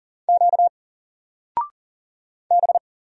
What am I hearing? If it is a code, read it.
Morse code, heard as a keyed tone, about 39 wpm, 700 Hz. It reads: Q B
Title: Single Letter-Number 40wpm